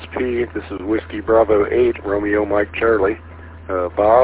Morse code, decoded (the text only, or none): none